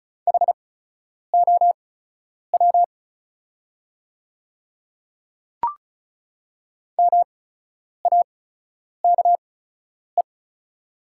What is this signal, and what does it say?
Morse code, heard as a keyed tone, about 36 wpm, spaced as for 10 wpm, 700 Hz. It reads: HOW MAKE
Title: 8x Character Spacing - Top 100 Words - 35wpm